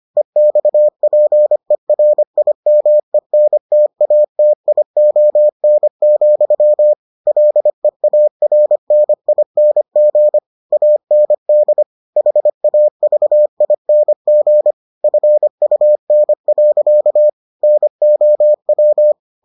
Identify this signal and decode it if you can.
Morse code, heard as a keyed tone, about 25 wpm, 600 Hz. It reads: EXPERIMENTATION, LEARNING AND HAVING FUN. NOW